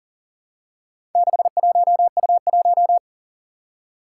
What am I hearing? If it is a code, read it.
Morse code, heard as a keyed tone, about 40 wpm, 700 Hz. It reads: 61U1
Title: Four Challenging Letter-Number Combinations 40wpm